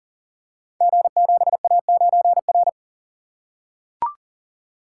Morse code, heard as a keyed tone, about 40 wpm, 700 Hz. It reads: G7A9R
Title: Code Groups - 5 Letter:Numbers - 40wpm